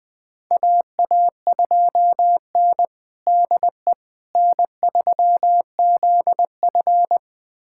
Morse code, heard as a keyed tone, about 20 wpm, 700 Hz. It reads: AA2N DE N3ZF
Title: USA Callsign DE Callsign - 4 Characters - 20wpm